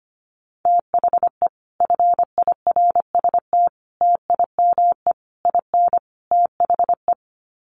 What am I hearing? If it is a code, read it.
Morse code, heard as a keyed tone, about 25 wpm, 700 Hz. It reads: THE FIRST TIME IN THE
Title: Sets of 5 Words Encore 25wpm